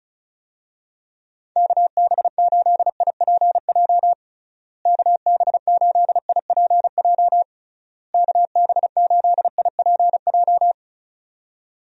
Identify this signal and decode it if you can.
Morse code, heard as a keyed tone, about 36 wpm, 700 Hz. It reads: KB8IPJ KB8IPJ KB8IPJ
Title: USA Call Signs - 6 Characters - Repeated 3x - 35wpm